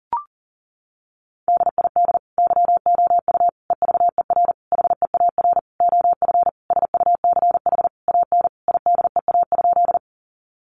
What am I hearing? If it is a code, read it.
Morse code, heard as a keyed tone, about 40 wpm, 700 Hz. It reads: DID YOU EVER HEAR OF SUCH AN IDEA?
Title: Sentences from Top 400 Words 40wpm